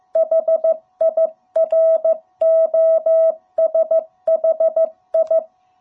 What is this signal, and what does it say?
Morse code, heard as a keyed tone, about 15 wpm, 660 Hz. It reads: HIROSHI
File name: jh8cbh / 2021cw